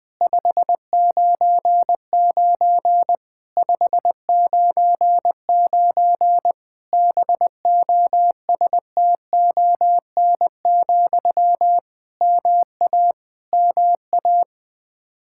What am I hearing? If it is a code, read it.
Morse code, heard as a keyed tone, about 20 wpm, 700 Hz. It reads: 599 599 BOSTON, MA MA
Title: Practice QSO - Exchange 1 - Beginning - 20wpm